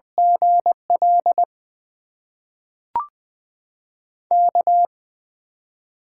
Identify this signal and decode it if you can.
Morse code, heard as a keyed tone, about 20 wpm, 700 Hz. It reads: GL K
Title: USA Call Signs - 6 Characters - 8x Character Spacing - 20wpm